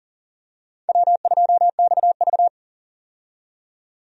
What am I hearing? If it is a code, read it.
Morse code, heard as a keyed tone, about 40 wpm, 700 Hz. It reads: W2XV
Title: USA Call Signs - 4 Characters - 40wpm